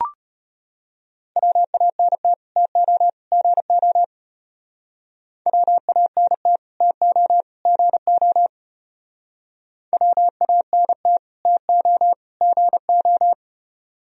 Morse code, heard as a keyed tone, about 33 wpm, 700 Hz. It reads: WANT TO GO WANT TO GO WANT TO GO